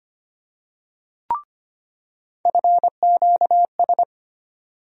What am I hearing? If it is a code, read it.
Morse code, heard as a keyed tone, about 25 wpm, 700 Hz. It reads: FQS
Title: USA Call Sign Postfixes - 3 Letters 25wpm